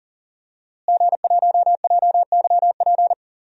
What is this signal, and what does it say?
Morse code, heard as a keyed tone, about 40 wpm, 700 Hz. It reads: G1JYP
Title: DXCC-100 Call Signs - 5 Characters - Repeated 3x - 40wpm